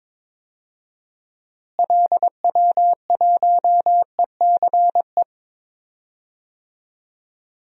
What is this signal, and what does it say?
Morse code, heard as a keyed tone, about 21 wpm, 700 Hz. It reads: LW1ECE
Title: DXCC-100 Call Signs - 6 Characters - Repeated 3x - 22wpm